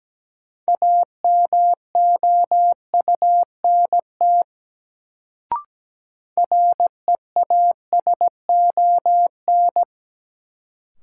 Morse code, heard as a keyed tone, about 17 wpm, 700 Hz. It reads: AMOUNT REASON
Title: Sentences from Top 700 Words - New Word Drill - 17wpm